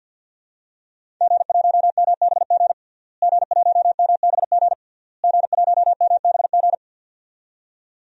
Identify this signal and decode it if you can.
Morse code, heard as a keyed tone, about 51 wpm, 700 Hz. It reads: G1MBG G1MBG G1MBG
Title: DXCC-100 Call Signs - 5 Characters - Repeated 3x - 50wpm